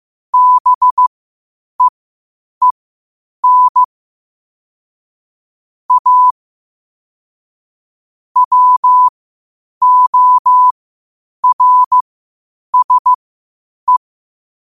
Morse code, heard as a keyed tone, about 15 wpm, spaced as for 8 wpm, 1000 Hz. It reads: BEEN A WORSE
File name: qotdfiles / 8wpm